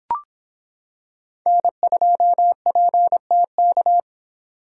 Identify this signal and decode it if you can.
Morse code, heard as a keyed tone, about 25 wpm, 700 Hz. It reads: N2PTK